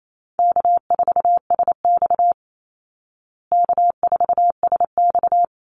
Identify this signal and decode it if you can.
Morse code, heard as a keyed tone, about 28 wpm, 700 Hz. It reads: K4SX K4SX
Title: Call Signs Repeated 2x - 28wpm